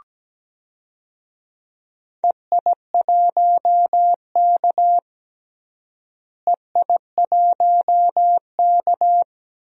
Morse code, heard as a keyed tone, about 17 wpm, 700 Hz. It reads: EI1K EI1K
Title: DXCC-100 Call Signs - 4 Characters - Repeated 2x - 17wpm